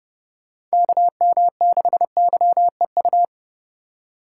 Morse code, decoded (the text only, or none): KM6YEU